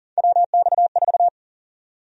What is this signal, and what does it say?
Morse code, heard as a keyed tone, about 40 wpm, 700 Hz. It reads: WX4
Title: USA Call Sign Prefixes - 40wpm